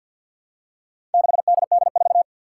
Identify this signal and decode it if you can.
Morse code, heard as a keyed tone, about 53 wpm, 700 Hz. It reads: 6DD4
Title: Four Challenging Letter-Number Combinations 50wpm